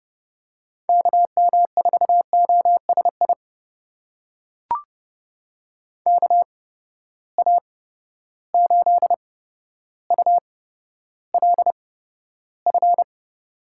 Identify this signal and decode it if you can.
Morse code, heard as a keyed tone, about 30 wpm, spaced as for 13 wpm, 700 Hz. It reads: KM4OSI KA8ULF
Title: USA Call Signs - 6 Characters - 8x Character Spacing - 30wpm